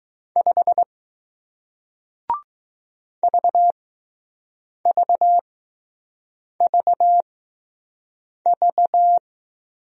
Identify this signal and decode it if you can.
Morse code, heard as a keyed tone, about 20 wpm, spaced as for 6 wpm, 700 Hz. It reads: 5 VVVV